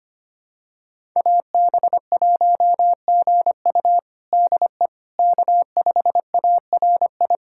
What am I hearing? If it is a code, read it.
Morse code, heard as a keyed tone, about 25 wpm, 700 Hz. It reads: AB1GU DE K5ARI
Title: USA Callsign DE Callsign - 5 Characters - 25wpm